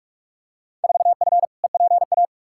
Morse code, heard as a keyed tone, about 46 wpm, 700 Hz. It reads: 4F EPA
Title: Field Day Exchanges - 45wpm